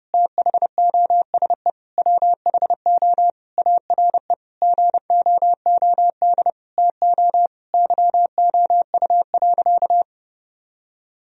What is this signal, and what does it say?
Morse code, heard as a keyed tone, about 32 wpm, 700 Hz. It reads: THOSE WHO ARE GOOD TO YOU.
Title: Sentences from Top 1000 Words - Review Part 1 - 30wpm